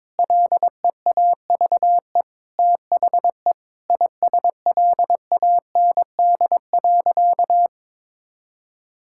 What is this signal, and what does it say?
Morse code, heard as a keyed tone, about 22 wpm, 700 Hz. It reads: LEAVE THE ISLAND.
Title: Sentences from Top 1000 Words - Review Part 6 - 22wpm